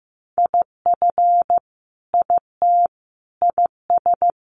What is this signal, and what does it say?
Morse code, heard as a keyed tone, about 15 wpm, 700 Hz. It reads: IF IT IS
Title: Sentences from Top 1100 Words 15wpm